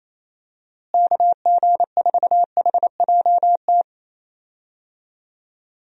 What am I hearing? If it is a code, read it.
Morse code, heard as a keyed tone, about 28 wpm, 700 Hz. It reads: KG4HJT
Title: USA Call Signs - 6 Characters - Repeated 3x - 28wpm